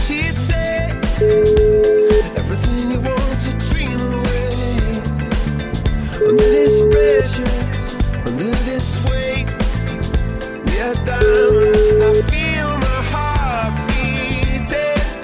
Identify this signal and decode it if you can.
Ringback tone (Europe, 425 Hz), with noise.